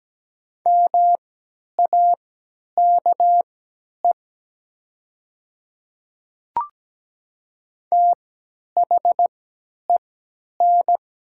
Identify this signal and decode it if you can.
Morse code, heard as a keyed tone, about 17 wpm, spaced as for 10 wpm, 700 Hz. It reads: MAKE THEN
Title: 3x Character Spacing - Top 100 Words - 17wpm